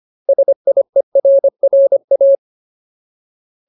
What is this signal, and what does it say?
Morse code, heard as a keyed tone, about 25 wpm, 550 Hz. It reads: SIERRA
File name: Ultimate Word List / 25WPM Ultimate